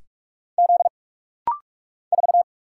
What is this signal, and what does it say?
Morse code, heard as a keyed tone, about 46 wpm, 700 Hz. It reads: Z 4
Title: Single Letter-Number Alt - Rapid-Fire 45wpm